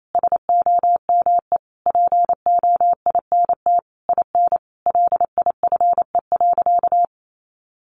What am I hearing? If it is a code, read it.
Morse code, heard as a keyed tone, about 28 wpm, 700 Hz. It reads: SOME POINT IN LIFE.
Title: Sentences from Top 1200 Words 28wpm